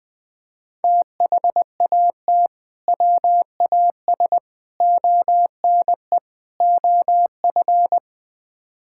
Morse code, heard as a keyed tone, about 21 wpm, 700 Hz. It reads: THAT WAS ONE OF
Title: Build A Phrase - 5-Word Phrases - 20wpm